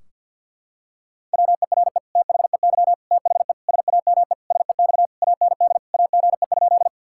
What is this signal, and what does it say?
Morse code, heard as a keyed tone, about 48 wpm, 700 Hz. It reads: WERE THEY THE SAME SEX AND AGE?
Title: Sentences from Top 1600 Words 50wpm